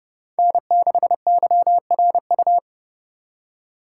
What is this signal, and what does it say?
Morse code, heard as a keyed tone, about 30 wpm, 700 Hz. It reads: N6YRU
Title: Speed Racing - USA 1x3 Call Signs 30wpm